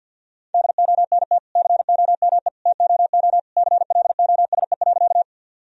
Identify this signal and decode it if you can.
Morse code, heard as a keyed tone, about 50 wpm, 700 Hz. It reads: DONT COME TOO CLOSE.
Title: Sentences from Top 1000 Words - Review Part 1 - 50wpm